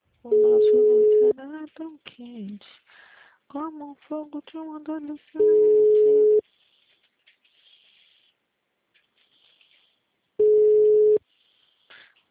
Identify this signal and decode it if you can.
Ringback tone (Europe, 425 Hz), with speech.